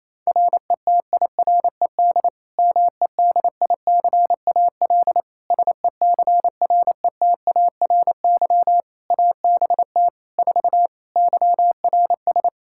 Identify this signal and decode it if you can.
Morse code, heard as a keyed tone, about 28 wpm, 700 Hz. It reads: RETIRED MEDICAL SECRETARY ABT 4 YRS
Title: Practice QSO - Exchange 3 - More Info - 28wpm